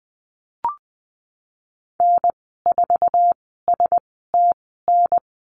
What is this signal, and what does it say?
Morse code, heard as a keyed tone, about 20 wpm, spaced as for 14 wpm, 700 Hz. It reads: N4STN
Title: 055 Lesson 10g - Letter r - Call Signs 20wpm